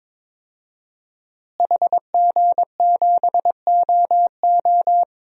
Morse code, heard as a keyed tone, about 23 wpm, 700 Hz. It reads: HG7OO
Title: DXCC-100 Call Signs - 5 Characters - Repeated 3x - 22wpm